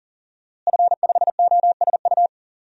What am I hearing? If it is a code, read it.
Morse code, heard as a keyed tone, about 40 wpm, 700 Hz. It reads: F5OSU
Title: DXCC-100 Call Signs - 5 Characters - Repeated 2x - 40wpm